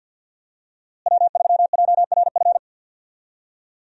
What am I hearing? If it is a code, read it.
Morse code, heard as a keyed tone, about 51 wpm, 700 Hz. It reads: W3JRF
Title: USA Call Signs - 5 Characters - Repeated 2x - 50wpm